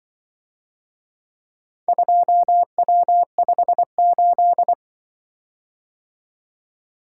Morse code, heard as a keyed tone, about 24 wpm, 700 Hz. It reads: 2W58
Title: Speed-Racing - Four Letter-Number Combinations - 20wpm